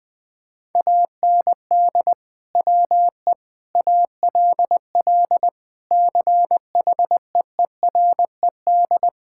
Morse code, heard as a keyed tone, about 20 wpm, 700 Hz. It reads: AND WE ALL CHEERED
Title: Sentences with 2-Word Phrases - 20wpm